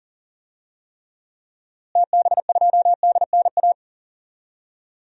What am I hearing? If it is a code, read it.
Morse code, heard as a keyed tone, about 40 wpm, 700 Hz. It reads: TB2DNA
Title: DXCC-100 Call Signs - 6 Characters - Repeated 2x - 40wpm